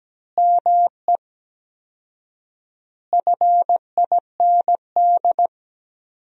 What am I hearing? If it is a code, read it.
Morse code, heard as a keyed tone, about 17 wpm, 700 Hz. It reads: ME FIND
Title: 4x Word-Spacing - Sentences from Top 200 Words - 17wpm